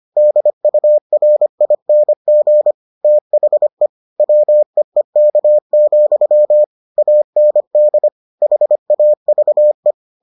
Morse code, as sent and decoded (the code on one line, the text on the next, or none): -.. ..- .-. .. -. --. / - .... / .-- . -.- --..-- / .- -. -.. / .... .- ...- .
DURING THE WEEK, AND HAVE